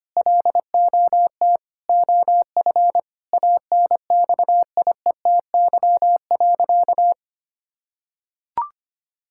Morse code, heard as a keyed tone, about 25 wpm, 700 Hz. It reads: LOT OF ANXIETY.
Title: Sentences from Top 1600 Words 25wpm